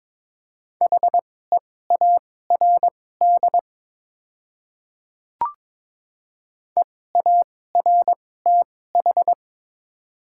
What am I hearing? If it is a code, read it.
Morse code, heard as a keyed tone, about 22 wpm, spaced as for 16 wpm, 700 Hz. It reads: HEARD EARTH
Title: 2x Character Spacing - Top 500 Words - 22wpm